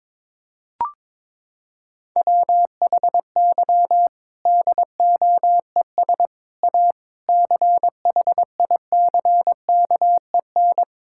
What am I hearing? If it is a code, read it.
Morse code, heard as a keyed tone, about 22 wpm, 700 Hz. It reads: WHY DOES A CHICKEN
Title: Sentences from Top 1000 Words - Review Part 5 - 22wpm